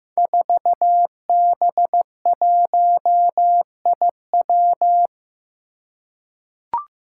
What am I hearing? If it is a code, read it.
Morse code, heard as a keyed tone, about 15 wpm, 700 Hz. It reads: 4B1IW
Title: Five Challenging Letter-Number Combinations 15wpm